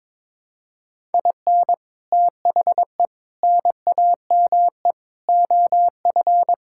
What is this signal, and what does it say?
Morse code, heard as a keyed tone, about 21 wpm, 700 Hz. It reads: IN THE NAME OF
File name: practice / 22wpm / Sentences with 4-Word Phrases - 22wpm